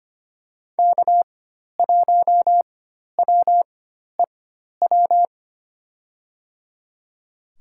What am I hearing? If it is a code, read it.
Morse code, heard as a keyed tone, about 25 wpm, spaced as for 12 wpm, 700 Hz. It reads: K1WEW
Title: USA Call Signs - 5 Characters - 4x Character Spacing - 25wpm